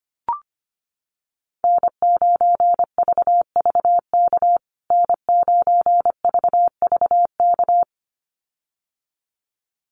Morse code, heard as a keyed tone, about 25 wpm, 700 Hz. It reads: N9VVK N9VVK
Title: USA Call Signs Repeated 2x 25wpm